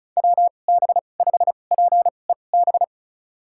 Morse code, heard as a keyed tone, about 34 wpm, spaced as for 25 wpm, 700 Hz. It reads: WB5PEB
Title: USA Call Signs - 6 Characters - 2x Character Spacing - 35wpm